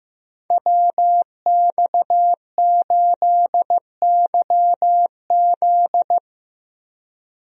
Morse code, heard as a keyed tone, about 15 wpm, 700 Hz. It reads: WX8YZ
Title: USA Call Signs - 5 Characters - Repeated 2x - 15wpm